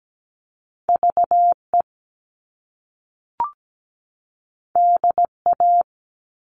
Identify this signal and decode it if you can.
Morse code, heard as a keyed tone, about 17 wpm, 700 Hz. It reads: VE DA
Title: All Cut Numbers - 2-Digit 17wpm